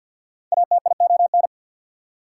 Morse code, heard as a keyed tone, about 69 wpm, spaced as for 59 wpm, 700 Hz. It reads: ATION
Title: Speed-Racing Prefixes and Suffixes 50wpm